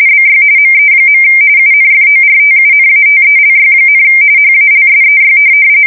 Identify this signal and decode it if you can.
RTTY, about 45.45 baud. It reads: BY ME. UNFORTUNATELY, HALSEY
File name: fsk170 / USS-New-Jersey